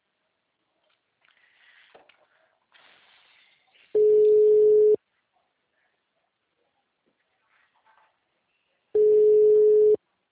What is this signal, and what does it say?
Ringback tone (Europe, 425 Hz).
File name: out- -2002- - - .44837